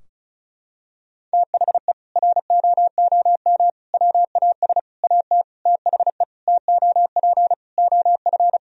Morse code, heard as a keyed tone, about 36 wpm, 700 Hz. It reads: THE ROOM WAS AT THE TOP OF
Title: Sentences from Top 1000 Words - Review Part 2 - 35wpm